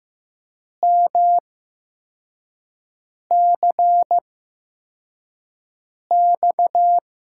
Morse code, heard as a keyed tone, about 15 wpm, spaced as for 4 wpm, 700 Hz. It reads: MCX
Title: USA Call Signs - 5 Characters - 8x Character Spacing - 15wpm